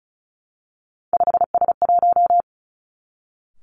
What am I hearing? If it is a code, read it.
Morse code, heard as a keyed tone, about 36 wpm, 700 Hz. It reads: 5S1
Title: Three Challenging Letter-Number Combinations 35wpm